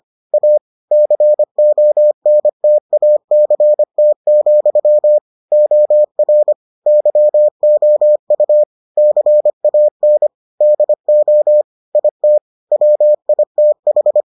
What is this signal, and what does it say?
Morse code, heard as a keyed tone, about 25 wpm, 600 Hz. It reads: A CONTACT, OR YOU CAN DO IT WITH